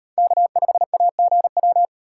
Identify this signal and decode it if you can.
Morse code, heard as a keyed tone, about 38 wpm, 700 Hz. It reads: K5AGW